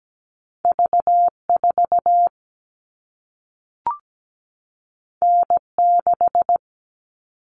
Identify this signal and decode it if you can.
Morse code, heard as a keyed tone, about 17 wpm, 700 Hz. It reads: V4 N6
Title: All Cut Numbers - 2-Digit 17wpm